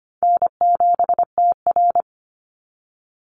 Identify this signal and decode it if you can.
Morse code, heard as a keyed tone, about 25 wpm, 700 Hz. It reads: N7TR
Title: Easy Call Signs - 25wpm